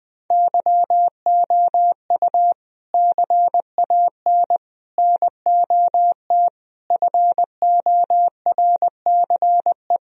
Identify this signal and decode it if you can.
Morse code, heard as a keyed tone, about 20 wpm, 700 Hz. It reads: YOU CAN NOT FORCE
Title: Sentences from Top 1000 Words - Review Part 3 - 20wpm